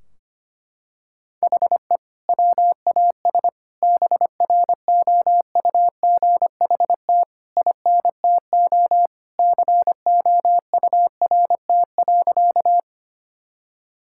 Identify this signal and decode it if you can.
Morse code, heard as a keyed tone, about 25 wpm, 700 Hz. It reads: HE WAS BROUGHT INTO COURT.
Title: Sentences from Top 1600 Words 25wpm